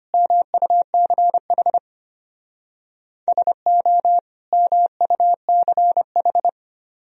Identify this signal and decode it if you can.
Morse code, heard as a keyed tone, about 33 wpm, 700 Hz. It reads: MUCH SO MUCH